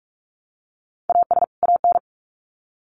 Morse code, heard as a keyed tone, about 44 wpm, 700 Hz. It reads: AS AN
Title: Sets of 2 Words 45wpm